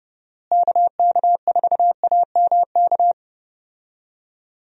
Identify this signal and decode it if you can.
Morse code, heard as a keyed tone, about 31 wpm, 700 Hz. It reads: KK4AMK
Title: USA Call Signs - 6 Characters - Repeated 3x - 30wpm